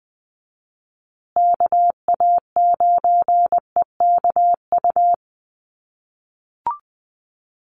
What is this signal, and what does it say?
Morse code, heard as a keyed tone, about 20 wpm, 700 Hz. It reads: KA9EKU
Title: USA Call Signs Repeated 2x 20wpm